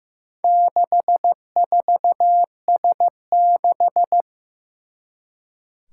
Morse code, heard as a keyed tone, about 15 wpm, 700 Hz. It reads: 64S6
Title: Four Challenging Letter-Number Combinations 15wpm